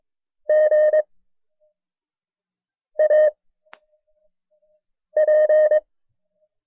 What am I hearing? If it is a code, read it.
Morse code, heard as a keyed tone, about 22 wpm, 610 Hz. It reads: G A P
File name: FreeFiles / CW-practice / KB6MT-tape1-side2